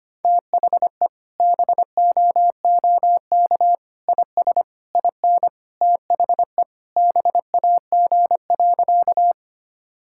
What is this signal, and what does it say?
Morse code, heard as a keyed tone, about 25 wpm, 700 Hz. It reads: THE BOOK IS IN THE BAG.
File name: practice / 25wpm / Sentences with 3-Word Phrases - 25wpm